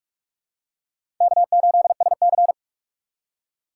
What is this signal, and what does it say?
Morse code, heard as a keyed tone, about 44 wpm, 700 Hz. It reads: K8SC
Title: USA Call Signs - 4 Characters - 6x Character Spacing - 45wpm